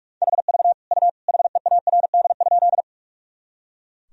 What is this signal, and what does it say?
Morse code, heard as a keyed tone, about 46 wpm, 700 Hz. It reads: HV U HEARD?
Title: Sentences from Top 400 Words and Abbreviations 45wpm